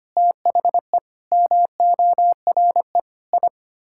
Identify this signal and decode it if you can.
Morse code, heard as a keyed tone, about 25 wpm, 700 Hz. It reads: THE MORE I
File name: Sets of 9 Words - 25wpm